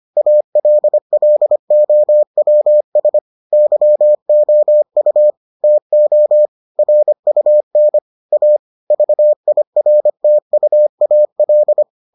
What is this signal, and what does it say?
Morse code, heard as a keyed tone, about 25 wpm, 600 Hz. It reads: ALLOWS YOU TO RUN A VIRTUAL